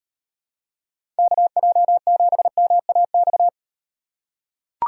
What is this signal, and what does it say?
Morse code, heard as a keyed tone, about 38 wpm, 700 Hz. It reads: KJ7MAX